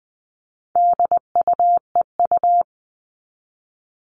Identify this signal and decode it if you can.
Morse code, heard as a keyed tone, about 20 wpm, 700 Hz. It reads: DUEU
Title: All Cut Numbers - 4-Digit 20wpm